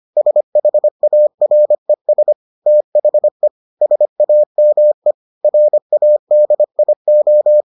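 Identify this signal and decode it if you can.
Morse code, heard as a keyed tone, about 25 wpm, 600 Hz. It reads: SHARES THE SAME RADIO
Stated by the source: Morse edition of the Foundations of Amateur Radio episodes.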